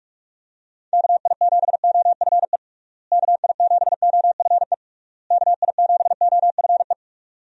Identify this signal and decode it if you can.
Morse code, heard as a keyed tone, about 42 wpm, 700 Hz. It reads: KI7OFE KI7OFE KI7OFE
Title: USA Call Signs - 6 Characters - Repeated 3x - 45wpm